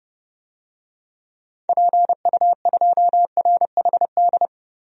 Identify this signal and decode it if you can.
Morse code, heard as a keyed tone, about 30 wpm, 700 Hz. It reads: PU2RHD
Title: DXCC-100 Call Signs - 6 Characters - 30wpm